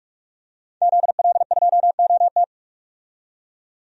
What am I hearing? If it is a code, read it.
Morse code, heard as a keyed tone, about 42 wpm, 700 Hz. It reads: ZL2OT
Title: DXCC-100 Call Signs - 5 Characters - 45wpm